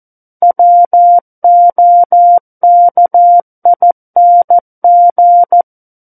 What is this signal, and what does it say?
Morse code, heard as a keyed tone, about 14 wpm, 700 Hz. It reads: WOKING
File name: Book One / 05-Chapter 5 - 14 WPM